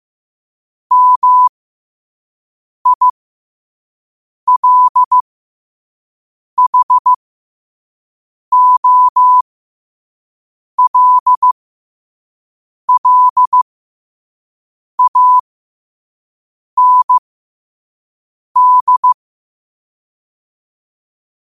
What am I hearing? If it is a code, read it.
Morse code, heard as a keyed tone, about 15 wpm, spaced as for 5 wpm, 1000 Hz. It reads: MILHOLLAND
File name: qotdfiles / 5wpm